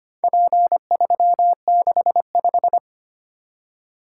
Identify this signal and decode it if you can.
Morse code, heard as a keyed tone, about 25 wpm, 700 Hz. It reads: P365
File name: practice / 25wpm / Four Letter-Number Combinations - 25wpm